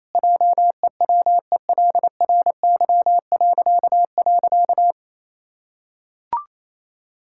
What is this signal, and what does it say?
Morse code, heard as a keyed tone, about 27 wpm, 700 Hz. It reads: JEWELRY..
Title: Pangram Challenge v2 - 28wpm